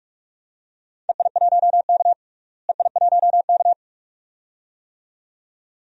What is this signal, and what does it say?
Morse code, heard as a keyed tone, about 41 wpm, 700 Hz. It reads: EI1K EI1K
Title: DXCC-100 Call Signs - 4 Characters - Repeated 2x - 45wpm